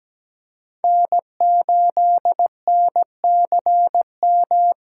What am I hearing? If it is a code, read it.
Morse code, heard as a keyed tone, about 17 wpm, 700 Hz. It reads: N8NCM
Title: USA Call Signs - 5 Characters - 6x Character Spacing - 17wpm